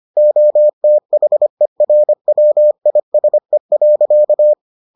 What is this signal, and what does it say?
Morse code, heard as a keyed tone, about 25 wpm, 600 Hz. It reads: OTHERWISE.